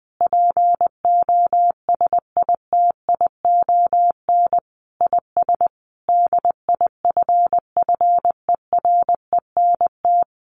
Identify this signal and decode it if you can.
Morse code, heard as a keyed tone, about 20 wpm, 700 Hz. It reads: POSITION IS DIFFERENT
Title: Sentences from Top 1400 Words 20wpm